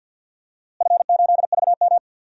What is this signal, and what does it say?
Morse code, heard as a keyed tone, about 53 wpm, 700 Hz. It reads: F8VM
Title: DXCC-100 Call Signs - 4 Characters - Repeated 3x - 50wpm